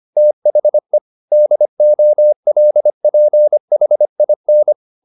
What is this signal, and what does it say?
Morse code, heard as a keyed tone, about 25 wpm, 600 Hz. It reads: THE DOLPHIN